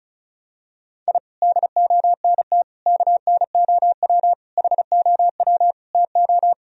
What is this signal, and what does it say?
Morse code, heard as a keyed tone, about 36 wpm, 700 Hz. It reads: I DONT KNOW HOW TO
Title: Sentences from Top 1000 Words - Review Part 1 - 35wpm